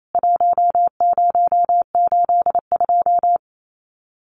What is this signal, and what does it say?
Morse code, heard as a keyed tone, about 28 wpm, 700 Hz. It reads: 1082
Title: CWT Exchange 28wpm